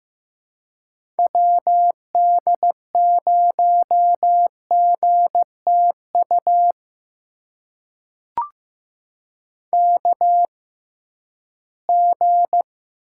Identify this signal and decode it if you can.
Morse code, heard as a keyed tone, about 15 wpm, spaced as for 11 wpm, 700 Hz. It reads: WD0GTU KG
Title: USA Call Signs - 6 Characters - 6x Character Spacing - 15wpm